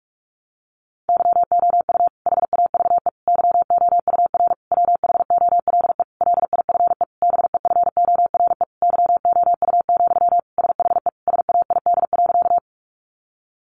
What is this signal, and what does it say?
Morse code, heard as a keyed tone, about 44 wpm, 700 Hz. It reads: YOU HAVE YOUR WHOLE LIFE BEFORE YOU, SHE SAID.
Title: Sentences from Top 400 Words 45wpm